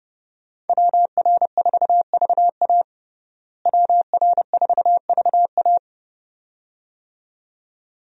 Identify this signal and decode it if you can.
Morse code, heard as a keyed tone, about 30 wpm, 700 Hz. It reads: WR4VA WR4VA
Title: USA Call Signs - 5 Characters - Repeated 2x - 30wpm